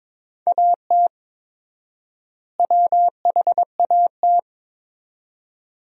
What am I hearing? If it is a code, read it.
Morse code, heard as a keyed tone, about 22 wpm, 700 Hz. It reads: AT WHAT
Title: 4x Word-Spacing - Sentences from Top 100 Words - 22wpm